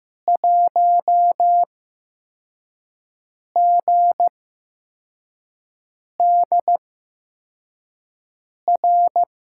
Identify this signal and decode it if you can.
Morse code, heard as a keyed tone, about 15 wpm, spaced as for 4 wpm, 700 Hz. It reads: 1GDR
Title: USA Call Signs - 6 Characters - 8x Character Spacing - 15wpm